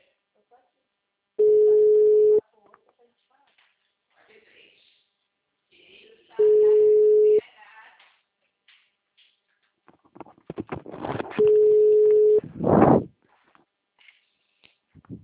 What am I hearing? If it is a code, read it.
Ringback tone (Europe, 425 Hz), with speech.
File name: out- -2001